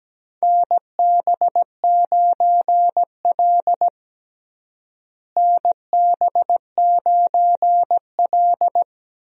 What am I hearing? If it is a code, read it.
Morse code, heard as a keyed tone, about 17 wpm, 700 Hz. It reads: NB9L NB9L
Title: USA Call Signs - 4 Characters - Repeated 2x - 17wpm